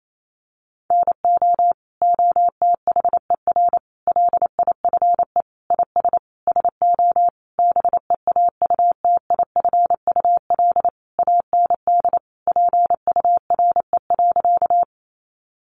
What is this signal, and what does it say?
Morse code, heard as a keyed tone, about 28 wpm, 700 Hz. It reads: NO OTHER LIFE IS SO BEAUTIFUL AND PURE.
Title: Sentences from Top 1500 Words 28wpm